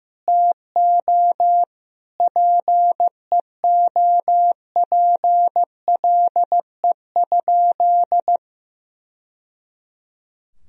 Morse code, heard as a keyed tone, about 15 wpm, 700 Hz. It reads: TO PEOPLE?
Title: Sentences from Top 1000 Words - Review Part 2 - 15wpm